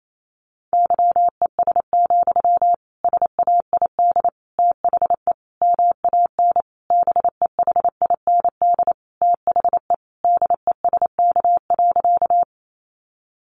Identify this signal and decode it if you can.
Morse code, heard as a keyed tone, about 28 wpm, 700 Hz. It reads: YES, SAID THE MAN BEHIND THE DESK.
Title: A Sound Of Thunder 28wpm